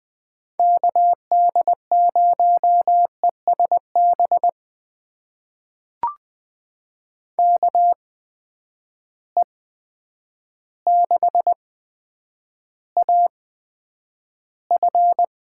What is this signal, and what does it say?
Morse code, heard as a keyed tone, about 20 wpm, spaced as for 9 wpm, 700 Hz. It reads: KD0ESB KE6AF
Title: USA Call Signs - 6 Characters - 8x Character Spacing - 20wpm